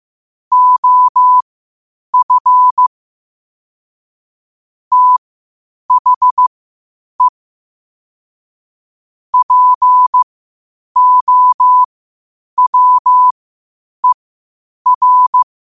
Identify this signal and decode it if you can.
Morse code, heard as a keyed tone, about 15 wpm, spaced as for 8 wpm, 1000 Hz. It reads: OF THE POWER
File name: qotdfiles / 8wpm